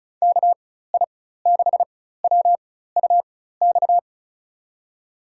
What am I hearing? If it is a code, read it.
Morse code, heard as a keyed tone, about 36 wpm, spaced as for 16 wpm, 700 Hz. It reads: KI6WUX
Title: USA Call Signs - 6 Characters - 4x Character Spacing - 35wpm